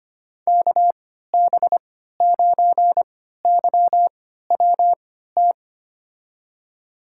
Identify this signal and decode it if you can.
Morse code, heard as a keyed tone, about 25 wpm, spaced as for 14 wpm, 700 Hz. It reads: KB9YWT
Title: USA Call Signs - 6 Characters - 3x Character Spacing - 25wpm